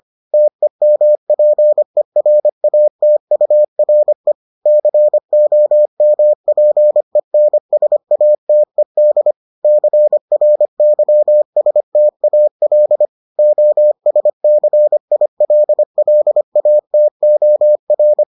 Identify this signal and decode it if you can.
Morse code, heard as a keyed tone, about 25 wpm, 600 Hz. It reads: TEMPERATURE COMPENSATED CRYSTAL OSCILLATOR